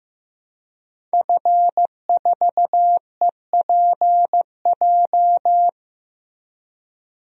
Morse code, heard as a keyed tone, about 16 wpm, 700 Hz. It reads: F4EPJ
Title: DXCC-100 Call Signs - 5 Characters - 15wpm